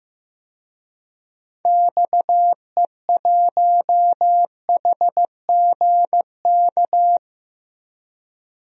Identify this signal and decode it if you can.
Morse code, heard as a keyed tone, about 15 wpm, 700 Hz. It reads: XE1HGK
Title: DXCC-100 Call Signs - 6 Characters - Repeated 3x - 15wpm